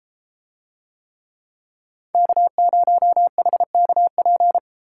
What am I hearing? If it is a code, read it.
Morse code, heard as a keyed tone, about 33 wpm, 700 Hz. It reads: K0HKP